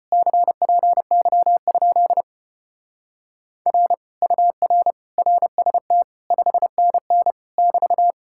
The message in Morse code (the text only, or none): CPY? R UR RST 5NN <BT>